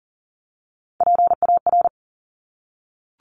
Morse code, heard as a keyed tone, about 41 wpm, 690 Hz. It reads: PAR
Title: Top Three Letter Combinations in Words 40wpm